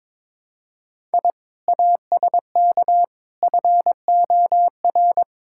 Morse code, heard as a keyed tone, about 22 wpm, 700 Hz. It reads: I ASK FOR
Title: Sentences from Top 1000 Words - Review Part 5 - 22wpm